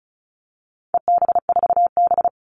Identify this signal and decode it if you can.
Morse code, heard as a keyed tone, about 34 wpm, 700 Hz. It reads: EB4B
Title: All Cut Numbers - 4-Digit 35wpm